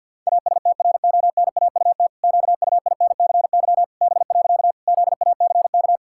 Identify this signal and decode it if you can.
Morse code, heard as a keyed tone, about 50 wpm, 700 Hz. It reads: ASTRONAUT QUINCY B. ZACK
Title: Pangram Challenge v2 - 50wpm